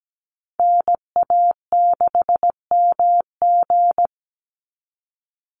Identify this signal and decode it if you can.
Morse code, heard as a keyed tone, about 17 wpm, 700 Hz. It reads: NA6MG
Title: NA SOTA Activator Callsigns 17wpm